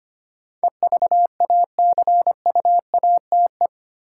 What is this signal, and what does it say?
Morse code, heard as a keyed tone, about 25 wpm, 700 Hz. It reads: EVACUATE
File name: 8 Letter Words - First 3 Letters Given - 25wpm